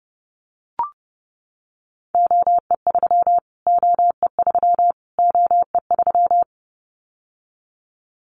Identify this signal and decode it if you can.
Morse code, heard as a keyed tone, about 30 wpm, 700 Hz. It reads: OE3 OE3 OE3
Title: Callsign Prefixes for Top 25 DXCC - Repeated 3x - 30wpm